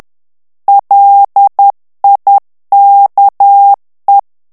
Morse code, heard as a keyed tone, about 10 wpm, 800 Hz. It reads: LIKE
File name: morse mp3 / ANGLAIS texte clair 2 vitesse 10 mots minute